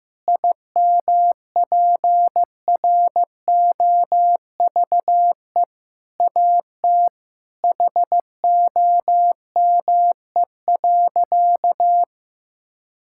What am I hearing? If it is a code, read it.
Morse code, heard as a keyed tone, about 15 wpm, 700 Hz. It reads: IMPROVE AT HOME.
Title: Sentences from Top 1000 Words - Review Part 5 - 15wpm